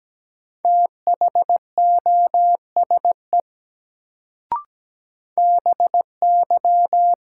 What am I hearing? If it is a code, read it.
Morse code, heard as a keyed tone, about 17 wpm, 700 Hz. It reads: THOSE BY
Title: Sets of 2 Words - New Word Drill - 17wpm